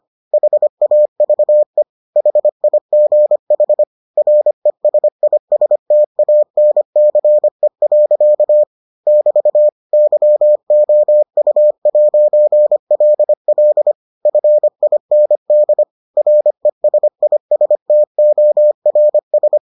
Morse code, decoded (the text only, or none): HAVE HIGH RESISTANCE. <BT> YOU'LL FIND RESISTORS